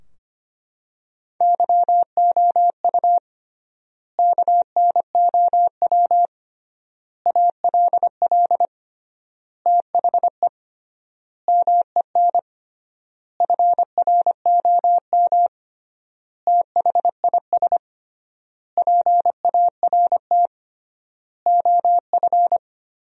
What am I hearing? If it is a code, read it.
Morse code, heard as a keyed tone, about 25 wpm, 700 Hz. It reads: YOU KNOW ALL THE MEN FROM THIS PART OF
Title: 3x Word-Spacing - Sentences from Top 300 Words - 25wpm